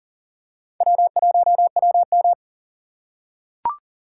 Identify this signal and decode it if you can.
Morse code, heard as a keyed tone, about 38 wpm, 700 Hz. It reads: W1WM
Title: USA Call Signs - 4 Characters - 4x Character Spacing - 40wpm